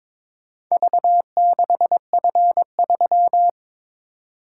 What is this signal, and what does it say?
Morse code, heard as a keyed tone, about 22 wpm, 700 Hz. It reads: V6F3
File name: practice / 22wpm / Four Letter-Number Combinations - 22wpm